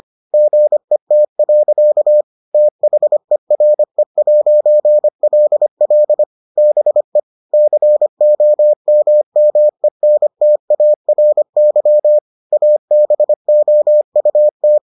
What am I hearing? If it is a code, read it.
Morse code, heard as a keyed tone, about 25 wpm, 600 Hz. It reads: GET. THERE'LL BE COMMENTARY ABOUT